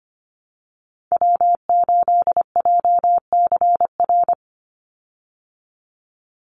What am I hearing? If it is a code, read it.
Morse code, heard as a keyed tone, about 25 wpm, 700 Hz. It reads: W8JCR
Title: USA Call Signs 25wpm